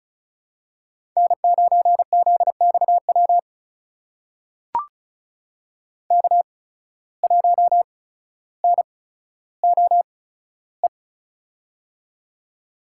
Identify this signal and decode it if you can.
Morse code, heard as a keyed tone, about 36 wpm, spaced as for 15 wpm, 700 Hz. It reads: N9ZXW K1NOE
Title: USA Call Signs - 5 Characters - 8x Character Spacing - 35wpm